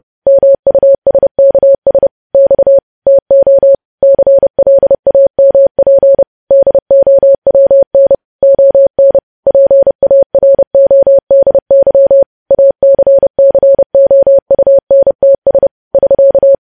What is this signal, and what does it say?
Morse code, heard as a keyed tone, about 30 wpm, 550 Hz. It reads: MUSKS X TO CLAMP DOWN ON PARODY ACCOUNTS <SK>
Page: news headlines in morse code at 30 wpm